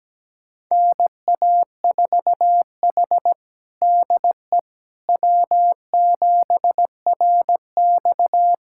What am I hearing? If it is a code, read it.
Morse code, heard as a keyed tone, about 17 wpm, 700 Hz. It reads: NA4H DE W7RX
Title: USA Callsign DE Callsign - 4 Characters - 17wpm